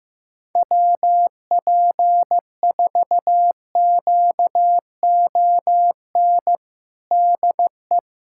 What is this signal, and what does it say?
Morse code, heard as a keyed tone, about 15 wpm, 700 Hz. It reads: WP4QON DE
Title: USA Callsign DE Callsign 15wpm